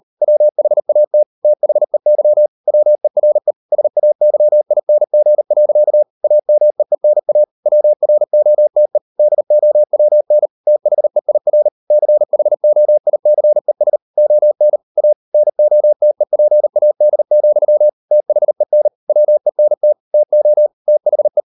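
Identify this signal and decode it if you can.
Morse code, heard as a keyed tone, about 38 wpm, 600 Hz. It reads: WHAT THEY WERE SAYING. AMEENA WROTE DOWN THEIR CHOICES ON A NOTEPAD, THEN WENT TO THE
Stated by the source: Morse code practice at 38 WPM, sending Ken Follett's novel Never in English, from CW med Gnister.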